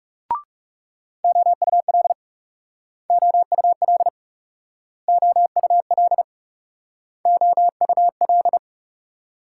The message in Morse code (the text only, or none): OUL OUL OUL OUL